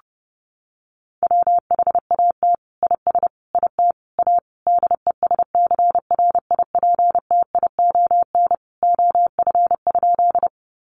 Morse code, heard as a keyed tone, about 30 wpm, 700 Hz. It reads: WHAT IS IT A DESCRIPTION OF?
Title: Sentences from Top 1300 Words 30wpm